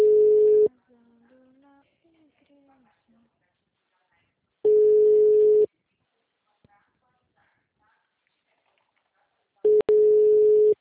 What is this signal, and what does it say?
Ringback tone (Europe, 425 Hz), with speech.
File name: out- -2006- - - .43469